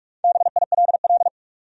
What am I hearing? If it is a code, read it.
Morse code, heard as a keyed tone, about 45 wpm, 700 Hz. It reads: BILL
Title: USA Common Names - 45wpm